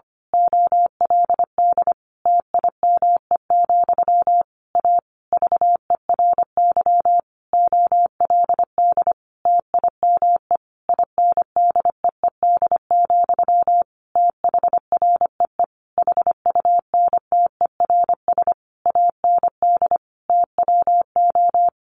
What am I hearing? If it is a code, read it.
Morse code, heard as a keyed tone, about 25 wpm, 700 Hz. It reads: OLD TIME, A VERY OLD TIME INDEED, THREE HUNTERS AND TWO
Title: A Sound Of Thunder 25wpm